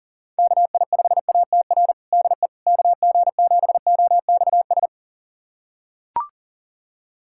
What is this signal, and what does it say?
Morse code, heard as a keyed tone, about 40 wpm, 700 Hz. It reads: KI5ATR DE KG7OXS
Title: USA Callsign DE Callsign 40wpm